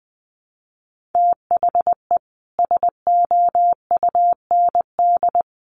Morse code, heard as a keyed tone, about 20 wpm, 700 Hz. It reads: THE SOUND
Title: Sets of 3 Words 20wpm